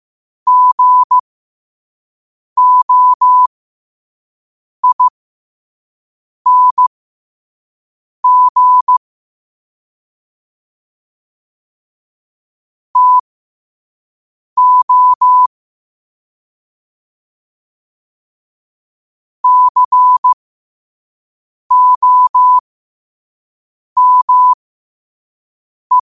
Morse code, heard as a keyed tone, about 15 wpm, spaced as for 5 wpm, 1000 Hz. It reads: GOING TO COME
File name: qotdfiles / 5wpm